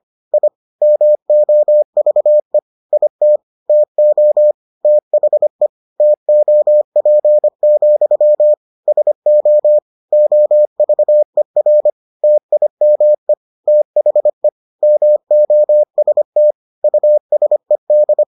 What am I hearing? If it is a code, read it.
Morse code, heard as a keyed tone, about 25 wpm, 600 Hz. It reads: I MOVE IT TO THE TOP, SO OVER TIME THE MOST USED